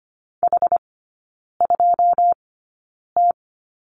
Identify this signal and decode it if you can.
Morse code, heard as a keyed tone, about 25 wpm, spaced as for 9 wpm, 700 Hz. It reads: H2T
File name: Lesson 17e - Letter u - Three Characters 25wpm